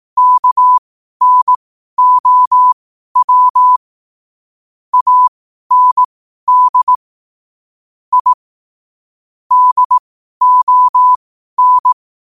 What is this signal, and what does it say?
Morse code, heard as a keyed tone, about 18 wpm, spaced as for 12 wpm, 1000 Hz. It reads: KNOW AND I DON
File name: qotdfiles / 12wpm